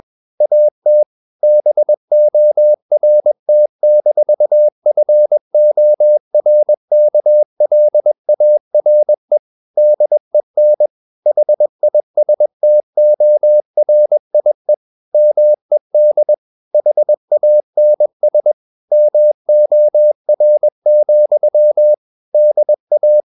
Morse code, heard as a keyed tone, about 21 wpm, 600 Hz. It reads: AT BORT-FORKLARE DEN HISTORIE MED HANS MOR, DA
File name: Ferie QRS 21wpm